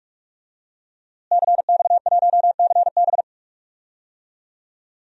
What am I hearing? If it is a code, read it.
Morse code, heard as a keyed tone, about 46 wpm, 700 Hz. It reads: CX1CB
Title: DXCC-100 Call Signs - 5 Characters - Repeated 3x - 45wpm